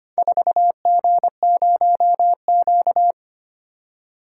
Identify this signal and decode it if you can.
Morse code, heard as a keyed tone, about 25 wpm, 700 Hz. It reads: 4G0Q
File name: practice / 25wpm / Four Letter-Number Combinations - 25wpm